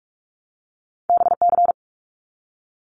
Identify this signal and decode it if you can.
Morse code, heard as a keyed tone, about 46 wpm, 700 Hz. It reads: BC
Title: Canadian Provinces and Territories 45wpm